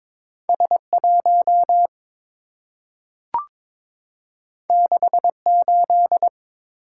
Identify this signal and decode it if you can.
Morse code, heard as a keyed tone, about 22 wpm, 700 Hz. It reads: S1 68
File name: practice / 22wpm / Two Letter-Number Combinations - 22wpm